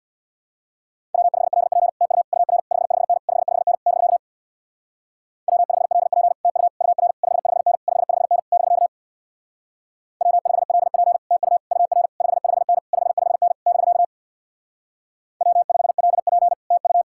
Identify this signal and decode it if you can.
Morse code, heard as a keyed tone, about 60 wpm, 700 Hz. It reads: W5LP TU UR 55N 55N <BK> W5LP TU UR 55N 55N <BK> W5LP TU UR 55N 55N <BK> W5LP TU
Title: Speed-Racing Common QSO Phrases - 50wpm